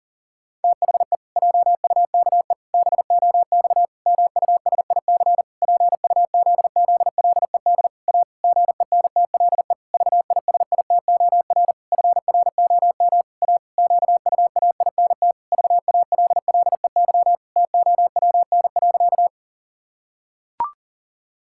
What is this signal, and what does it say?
Morse code, heard as a keyed tone, about 39 wpm, 700 Hz. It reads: THE JUKE BOX MUSIC PUZZLED A GENTLE VISITOR FROM A QUAINT VALLEY TOWN.
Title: Pangram Challenge v2 - 40wpm